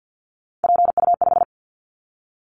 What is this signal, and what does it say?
Morse code, heard as a keyed tone, about 50 wpm, 700 Hz. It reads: PU5
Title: Callsign Prefixes for Top 100 DXCC - 50wpm